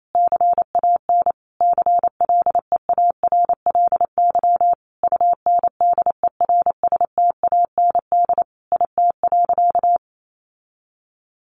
Morse code, heard as a keyed tone, about 28 wpm, 700 Hz. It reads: CAN CLEARLY UNDERSTAND IT.
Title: Sentences from Top 1400 Words 28wpm